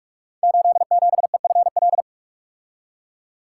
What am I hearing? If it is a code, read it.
Morse code, heard as a keyed tone, about 44 wpm, 700 Hz. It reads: 87EFL
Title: Code Groups - 5 Letter:Numbers - 45wpm